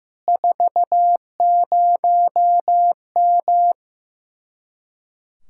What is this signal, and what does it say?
Morse code, heard as a keyed tone, about 15 wpm, 700 Hz. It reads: 40M
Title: Speed-Racing Common QSO Phrases Encore - 15wpm